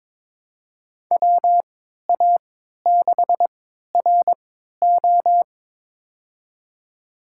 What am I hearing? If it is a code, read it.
Morse code, heard as a keyed tone, about 22 wpm, spaced as for 13 wpm, 700 Hz. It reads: WA6RO
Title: USA Call Signs - 3x Character Spacing - 22wpm